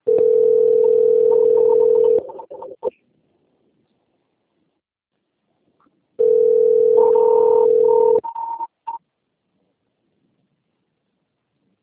Ringback tone (North America), 440 + 480 Hz.